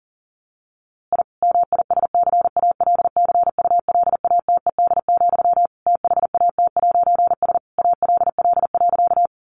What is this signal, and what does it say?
Morse code, heard as a keyed tone, about 40 wpm, 700 Hz. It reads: I MISCALCULATED, THAT'S ALL.
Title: A Sound Of Thunder 40wpm